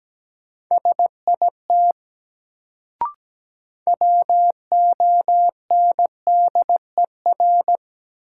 Morse code, heard as a keyed tone, about 17 wpm, 700 Hz. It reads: SIT WONDER
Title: Sentences from Top 500 Words - New Word Drill - 17wpm